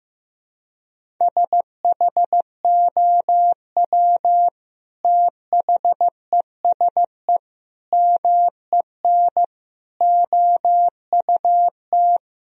Morse code, heard as a keyed tone, about 15 wpm, 700 Hz. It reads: SHOW THESE MEN OUT
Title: Sentences from Top 1000 Words - Review Part 1 - 15wpm